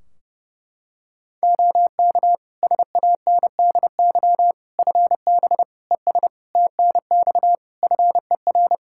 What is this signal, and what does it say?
Morse code, heard as a keyed tone, about 30 wpm, 700 Hz. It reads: OK SANDY FB ES TNX FER
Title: K1USN SST QSO Practice 30wpm